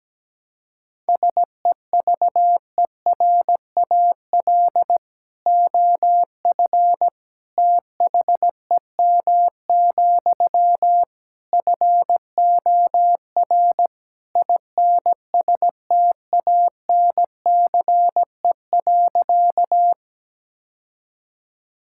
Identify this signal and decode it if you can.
Morse code, heard as a keyed tone, about 17 wpm, 700 Hz. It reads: SEVERAL OF THEM, FOR INSTANCE.
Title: Sentences from Top 1000 Words - Review Part 5 - 17wpm